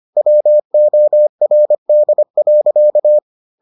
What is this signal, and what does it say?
Morse code, heard as a keyed tone, about 25 wpm, 600 Hz. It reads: WORD.